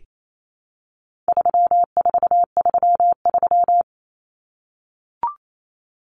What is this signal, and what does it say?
Morse code, heard as a keyed tone, about 28 wpm, 700 Hz. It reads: 3433
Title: Numbers - 4-Digit 28wpm